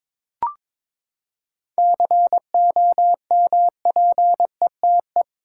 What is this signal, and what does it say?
Morse code, heard as a keyed tone, about 22 wpm, 700 Hz. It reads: COMPETE
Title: Seven Letter Words - 22wpm